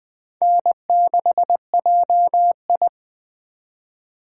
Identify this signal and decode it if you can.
Morse code, heard as a keyed tone, about 24 wpm, 700 Hz. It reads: N6JI
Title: Speed Racing - USA 1x2 Call Signs 20wpm